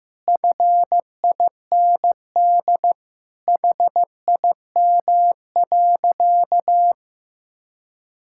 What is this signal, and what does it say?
Morse code, heard as a keyed tone, about 15 wpm, 700 Hz. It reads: FIND HIM.
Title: Sentences from Top 1000 Words - Review Part 5 - 15wpm